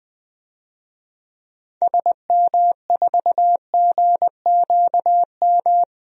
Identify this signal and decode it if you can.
Morse code, heard as a keyed tone, about 20 wpm, 700 Hz. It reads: SM4GQM
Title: DXCC-100 Call Signs - 6 Characters - 20wpm